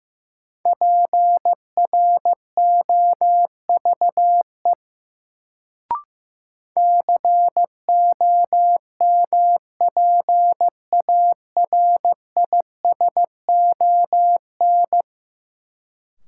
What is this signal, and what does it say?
Morse code, heard as a keyed tone, about 15 wpm, 700 Hz. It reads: PROVE COMPARISON
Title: Sentences from Top 1600 Words - New Word Drill - 15wpm